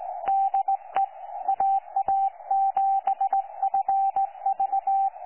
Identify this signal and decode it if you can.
Morse code, heard as a keyed tone, about 19 wpm, 780 Hz. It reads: DE AA7FV
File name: 12 39Z 18088.2kHz utah aa7fv k9tjl